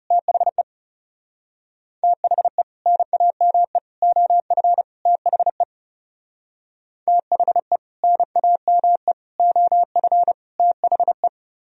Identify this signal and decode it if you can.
Morse code, heard as a keyed tone, about 37 wpm, 700 Hz. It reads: THE THE NAME OF THE THE NAME OF THE
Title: Speed-Racing - Sets of 4 Words 30wpm